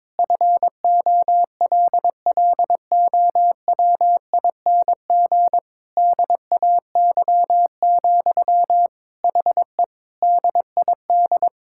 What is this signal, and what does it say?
Morse code, heard as a keyed tone, about 22 wpm, 700 Hz. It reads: FOLLOWING DAY, HE DID
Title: Sentences from Top 1000 Words - Review Part 6 - 22wpm